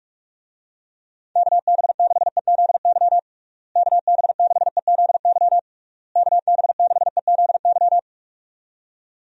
Morse code, heard as a keyed tone, about 46 wpm, 700 Hz. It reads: KB6EZY KB6EZY KB6EZY
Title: USA Call Signs - 6 Characters - Repeated 3x - 45wpm